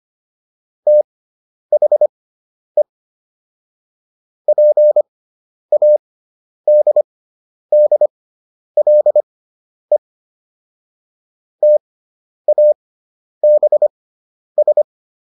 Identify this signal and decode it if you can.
Morse code, heard as a keyed tone, about 25 wpm, spaced as for 10 wpm, 600 Hz. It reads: THE PADDLE TABS